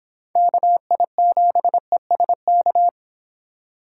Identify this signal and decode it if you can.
Morse code, heard as a keyed tone, about 26 wpm, 700 Hz. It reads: KI7ESK